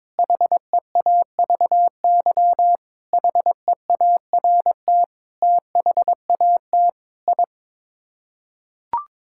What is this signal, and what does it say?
Morse code, heard as a keyed tone, about 22 wpm, 700 Hz. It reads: HEAVY HEART THAT I
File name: practice / 22wpm / Sets of 8 Words - 22wpm